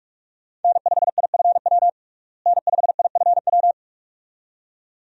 Morse code, heard as a keyed tone, about 46 wpm, 700 Hz. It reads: N5IFW N5IFW
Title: USA Call Signs - 5 Characters - Repeated 2x - 45wpm